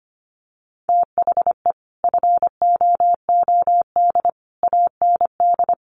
Morse code, heard as a keyed tone, about 25 wpm, 700 Hz. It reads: THE FOOD AND
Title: Sets of 5 Words 25wpm